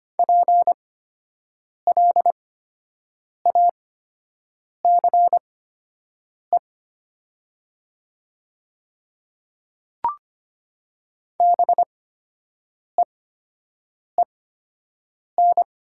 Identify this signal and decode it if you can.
Morse code, heard as a keyed tone, about 25 wpm, spaced as for 7 wpm, 700 Hz. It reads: PLACE BEEN
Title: 8x Character Spacing - Top 200 Words - 25wpm